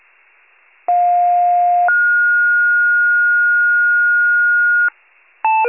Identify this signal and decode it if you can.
Two-tone page, heard as a one-second tone, then a three-second tone, about 710 then 1430 Hz. It is heard from 0.9 to 4.9 s.